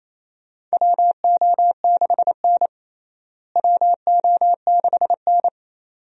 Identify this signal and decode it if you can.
Morse code, heard as a keyed tone, about 28 wpm, 700 Hz. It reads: WO6N WO6N
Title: USA Call Signs - 4 Characters - Repeated 2x - 28wpm